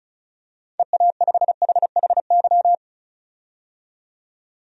Morse code, heard as a keyed tone, about 36 wpm, 700 Hz. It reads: EA5HHY
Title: DXCC-100 Call Signs - 6 Characters - 35wpm